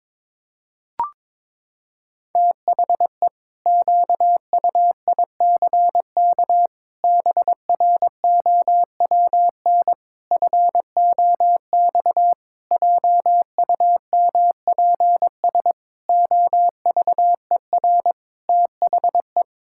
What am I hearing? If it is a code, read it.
Morse code, heard as a keyed tone, about 22 wpm, 700 Hz. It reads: THE QUICK BROWN FOX JUMPS OVER THE
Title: Pangram Challenge v2 - 22wpm